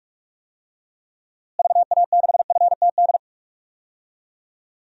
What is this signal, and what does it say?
Morse code, heard as a keyed tone, about 46 wpm, 700 Hz. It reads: VA6FTD
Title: DXCC-100 Call Signs - 6 Characters - Repeated 3x - 45wpm